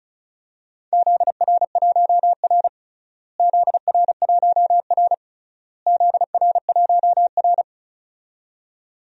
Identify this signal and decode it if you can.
Morse code, heard as a keyed tone, about 36 wpm, 700 Hz. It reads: ZR1R ZR1R ZR1R
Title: DXCC-100 Call Signs - 4 Characters - Repeated 3x - 35wpm